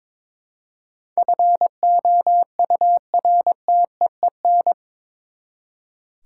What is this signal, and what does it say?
Morse code, heard as a keyed tone, about 22 wpm, 700 Hz. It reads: FOURTEEN
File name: practice / 22wpm / Eight Letter Words - 22wpm